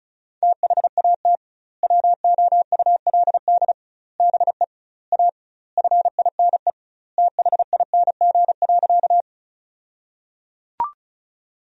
Morse code, heard as a keyed tone, about 34 wpm, 700 Hz. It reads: THAT WOULD BE A FINE THING.
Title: 2x Word-Spacing - Sentences from Top 500 Words - 35wpm